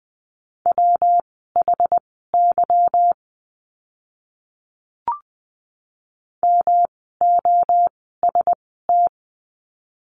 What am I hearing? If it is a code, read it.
Morse code, heard as a keyed tone, about 20 wpm, spaced as for 15 wpm, 700 Hz. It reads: WHY MOST
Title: 244 Lesson 39f - Letter z - Words 20wpm